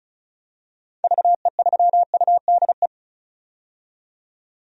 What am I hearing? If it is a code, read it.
Morse code, heard as a keyed tone, about 34 wpm, 700 Hz. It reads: VE3UDE
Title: DXCC-100 Call Signs - 6 Characters - 35wpm